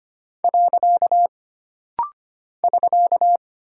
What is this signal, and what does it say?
Morse code, heard as a keyed tone, about 25 wpm, 700 Hz. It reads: . <SK>
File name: Single Punctuation-Prosign - Rapid-Fire - 25wpm